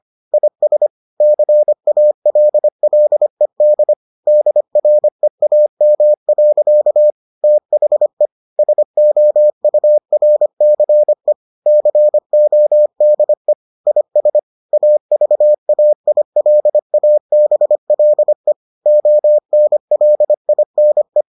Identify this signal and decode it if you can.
Morse code, heard as a keyed tone, about 25 wpm, 600 Hz. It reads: IS CALLED DREAM. THE SOURCE CODE IS AVAILABLE ONLINE